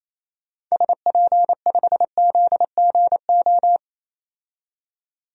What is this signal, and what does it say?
Morse code, heard as a keyed tone, about 28 wpm, 700 Hz. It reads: SP5ZGO
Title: DXCC-100 Call Signs - 6 Characters - 28wpm